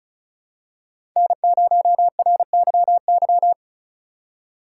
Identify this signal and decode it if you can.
Morse code, heard as a keyed tone, about 36 wpm, 700 Hz. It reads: N0RYY
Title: USA Call Signs - 8x Character Spacing - 35wpm